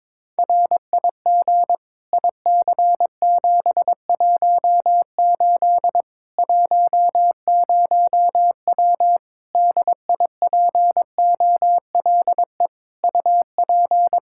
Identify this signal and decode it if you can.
Morse code, heard as a keyed tone, about 22 wpm, 700 Hz. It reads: RIG IC718 10W DIPOLE UP
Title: Practice QSO - Exchange 2 - Info - 22wpm